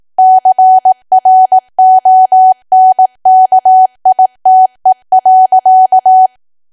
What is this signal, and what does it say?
Morse code, heard as a keyed tone, about 18 wpm, 750 Hz. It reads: CRONKITE.
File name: Archive / 18 WPM / 18WPM